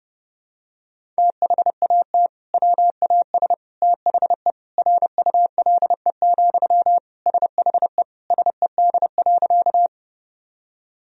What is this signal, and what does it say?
Morse code, heard as a keyed tone, about 30 wpm, 700 Hz. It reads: THAT WAS THE RULE, SHE SED.
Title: Sentences from Top 400 Words and Abbreviations 30wpm